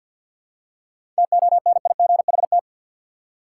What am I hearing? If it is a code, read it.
Morse code, heard as a keyed tone, about 49 wpm, 700 Hz. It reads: TONIGHT
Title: Seven Letter Words - 50wpm